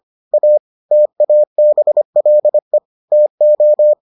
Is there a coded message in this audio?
Yes: Morse code from 0.3 to 3.9 s, heard as a keyed tone, about 25 wpm, 600 Hz. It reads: A TABLE TO